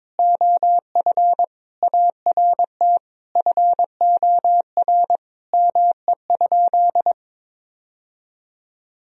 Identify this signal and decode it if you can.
Morse code, heard as a keyed tone, about 22 wpm, 700 Hz. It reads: OF ART FOR ME?
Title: Sentences from Top 1000 Words - Review Part 3 - 22wpm